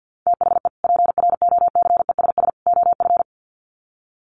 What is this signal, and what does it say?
Morse code, heard as a keyed tone, about 50 wpm, 700 Hz. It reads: THE PROCESS OF
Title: Sets of 3 Words 50wpm